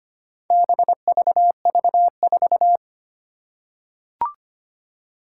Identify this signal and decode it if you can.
Morse code, heard as a keyed tone, about 25 wpm, 700 Hz. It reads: BVV4
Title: Four Challenging Letter-Number Combinations 25wpm